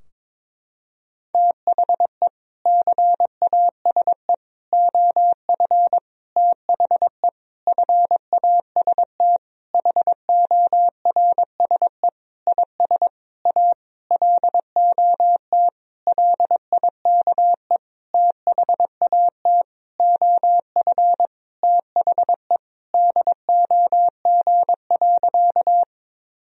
Morse code, heard as a keyed tone, about 22 wpm, 700 Hz. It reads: THE CASE OF THE FAST HORSE IS A LOT LIKE THAT OF THE DOG.
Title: Sentences from Top 1000 Words - Review Part 2 - 22wpm